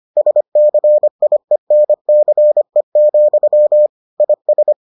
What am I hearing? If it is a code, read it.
Morse code, heard as a keyed tone, about 25 wpm, 600 Hz. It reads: SCIENCE, IS